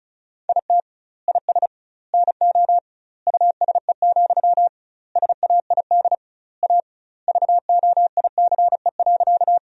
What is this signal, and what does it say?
Morse code, heard as a keyed tone, about 34 wpm, 700 Hz. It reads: IT IS NO USE, SAID A VOICE.
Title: 2x Word-Spacing - Sentences from Top 500 Words - 35wpm